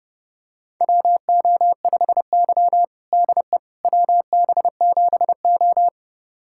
Morse code, heard as a keyed tone, about 30 wpm, 700 Hz. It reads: WO5Y DE WB7O
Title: USA Callsign DE Callsign - 4 Characters - 30wpm